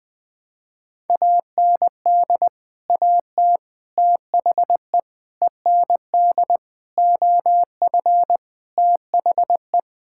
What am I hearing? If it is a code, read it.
Morse code, heard as a keyed tone, about 21 wpm, 700 Hz. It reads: AND AT THE END OF THE
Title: Sentences with 5-Word Phrases - 20wpm